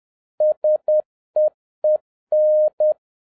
Morse code, heard as a keyed tone, about 10 wpm, 610 Hz. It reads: SEEN
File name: 2011-07-mp3 / n46 JOHN K3TN 10wpm